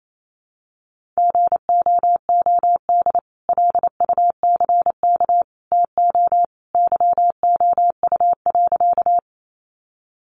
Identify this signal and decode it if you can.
Morse code, heard as a keyed tone, about 28 wpm, 700 Hz. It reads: GOOD LUCK TO YOU.
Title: Sentences from Top 1500 Words 28wpm